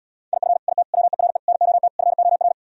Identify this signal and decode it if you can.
Morse code, heard as a keyed tone, about 77 wpm, 690 Hz. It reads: IF IT WERE NOT FOR